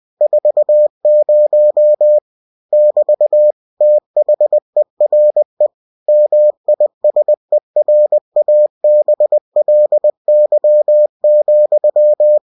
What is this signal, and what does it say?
Morse code, heard as a keyed tone, about 20 wpm, 600 Hz. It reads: 40 <BT> THERE MISERABLY,
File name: cw-books / winnie-the-pooh / winnie-20wpm-page-0040